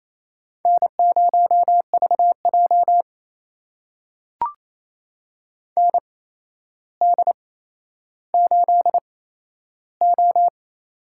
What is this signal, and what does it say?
Morse code, heard as a keyed tone, about 28 wpm, spaced as for 12 wpm, 700 Hz. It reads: N0VJ ND8O
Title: USA Call Signs - 4 Characters - 8x Character Spacing - 28wpm